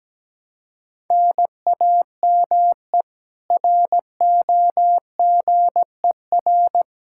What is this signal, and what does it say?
Morse code, heard as a keyed tone, about 17 wpm, 700 Hz. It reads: NAME ROGER
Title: Practice QSO - Exchange 1 - Beginning - 17wpm